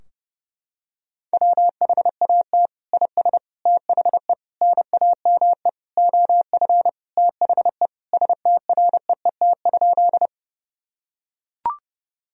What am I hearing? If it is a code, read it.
Morse code, heard as a keyed tone, about 30 wpm, 700 Hz. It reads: WHAT IS THE NAME OF THE STREET?
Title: Sentences from Top 1000 Words - Review Part 3 - 30wpm